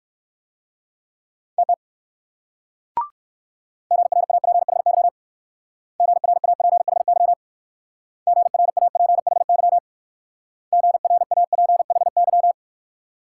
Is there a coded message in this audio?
Yes: Morse code from 1.6 to 12.5 s, heard as a keyed tone, about 58 wpm, 700 Hz. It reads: TT GRAPHY GRAPHY GRAPHY GRAPHY